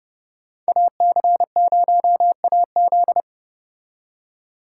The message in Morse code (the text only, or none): AC0AZ